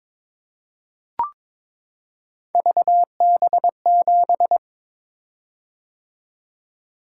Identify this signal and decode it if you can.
Morse code, heard as a keyed tone, about 22 wpm, 700 Hz. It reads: VB7
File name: practice / 22wpm / Three Letter-Number Combinations - 22wpm